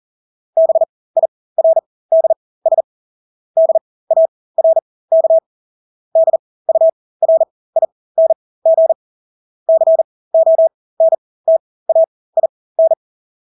Morse code, heard as a keyed tone, about 40 wpm, spaced as for 20 wpm, 650 Hz. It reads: BIRDS DARK DURING CONTAIN
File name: Engelske ord 40-20 wpm